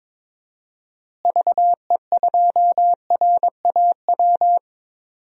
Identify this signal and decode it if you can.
Morse code, heard as a keyed tone, about 22 wpm, 700 Hz. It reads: VE2RAW
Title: DXCC-100 Call Signs - 6 Characters - Repeated 3x - 22wpm